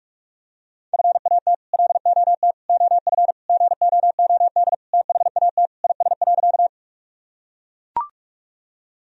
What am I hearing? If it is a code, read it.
Morse code, heard as a keyed tone, about 41 wpm, 700 Hz. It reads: FAT LOT OF GOOD THAT IS.
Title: Sentences from Top 1000 Words - Review Part 4 - 45wpm